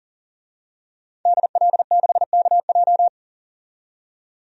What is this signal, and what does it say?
Morse code, heard as a keyed tone, about 40 wpm, 700 Hz. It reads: DL6KJ
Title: DXCC-100 Call Signs - 5 Characters - 40wpm